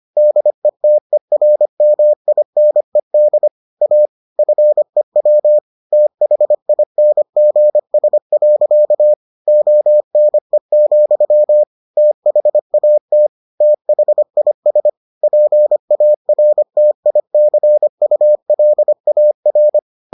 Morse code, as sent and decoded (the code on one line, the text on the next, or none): -.. - . .-. -- .. -. -.. / .- / ..-. .-- / - .... .. -. --. ... .-.-.- / --- -. --..-- / - .... .- - / - .... .. ... / .--. .- .-. - .. -.-. ..- .-.. .- .-.
DETERMINED A FEW THINGS. ONE, THAT THIS PARTICULAR